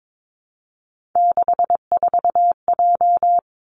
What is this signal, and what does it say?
Morse code, heard as a keyed tone, about 22 wpm, 700 Hz. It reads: 64J
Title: Three Challenging Letter-Number Combinations 22wpm